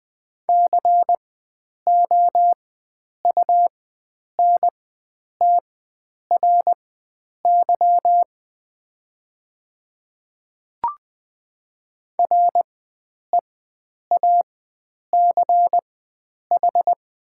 Morse code, heard as a keyed tone, about 20 wpm, spaced as for 9 wpm, 700 Hz. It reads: COUNTRY REACH
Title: 4x Character Spacing - Top 500 Words - 20wpm